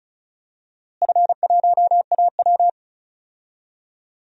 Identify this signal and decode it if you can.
Morse code, heard as a keyed tone, about 36 wpm, 700 Hz. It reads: F1AW
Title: DXCC-100 Call Signs - 4 Characters - Repeated 2x - 35wpm